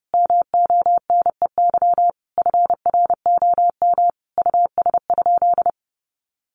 Morse code, heard as a keyed tone, about 30 wpm, 700 Hz. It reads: MONEY FROM US?
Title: Sentences from Top 1500 Words 30wpm